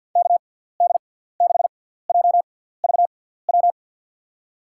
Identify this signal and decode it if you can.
Morse code, heard as a keyed tone, about 50 wpm, spaced as for 17 wpm, 700 Hz. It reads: KD6JVW
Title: USA Call Signs - 6 Characters - 6x Character Spacing - 50wpm